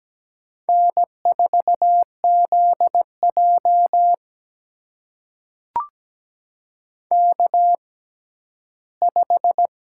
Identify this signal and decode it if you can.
Morse code, heard as a keyed tone, about 17 wpm, spaced as for 12 wpm, 700 Hz. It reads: N4ZJ K5
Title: USA Call Signs - 4 Characters - 6x Character Spacing - 17wpm